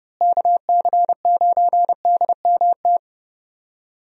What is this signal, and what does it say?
Morse code, heard as a keyed tone, about 30 wpm, 700 Hz. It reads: KC9DMT
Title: USA Call Signs - 6 Characters - Repeated 2x - 30wpm